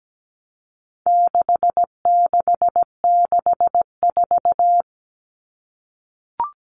Morse code, heard as a keyed tone, about 17 wpm, 700 Hz. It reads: 6664
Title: Numbers - 4-Digit 17wpm